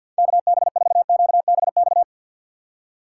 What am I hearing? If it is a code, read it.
Morse code, heard as a keyed tone, about 64 wpm, 700 Hz. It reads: KB4QBX